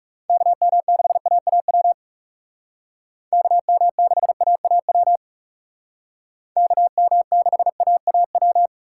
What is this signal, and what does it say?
Morse code, heard as a keyed tone, about 38 wpm, 700 Hz. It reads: KM6AAW KM6AAW KM6AAW